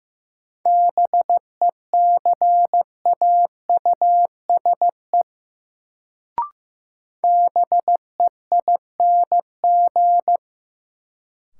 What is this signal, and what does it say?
Morse code, heard as a keyed tone, about 16 wpm, 700 Hz. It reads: BECAUSE BEING
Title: Sentences from Top 600 Words - New Word Drill - 15wpm